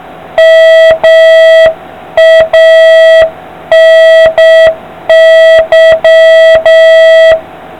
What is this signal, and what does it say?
Morse code, heard as a keyed tone, about 7 wpm, 650 Hz, with noise. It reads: MANY